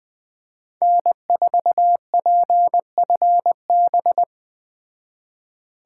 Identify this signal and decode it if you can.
Morse code, heard as a keyed tone, about 20 wpm, 700 Hz. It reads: N4PFB
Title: Speed Racing - USA Call Signs 17wpm